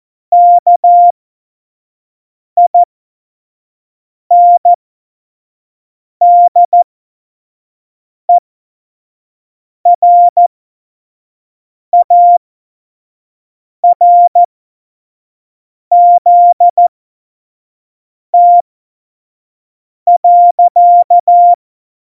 Morse code, heard as a keyed tone, about 14 wpm, spaced as for 5 wpm, 700 Hz. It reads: KINDERARZT.
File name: Morsekurs / Witze / Witz 16. Januar 2018, 25 F70